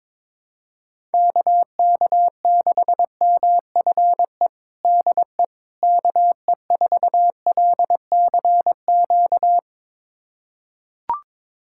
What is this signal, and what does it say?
Morse code, heard as a keyed tone, about 22 wpm, 700 Hz. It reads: KK6MFE DE KE4LCQ
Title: USA Callsign DE Callsign - 6 Characters - 22wpm